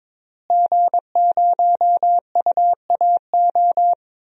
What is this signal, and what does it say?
Morse code, heard as a keyed tone, about 22 wpm, 700 Hz. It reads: G0UAO
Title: DXCC-100 Call Signs - 5 Characters - Repeated 2x - 22wpm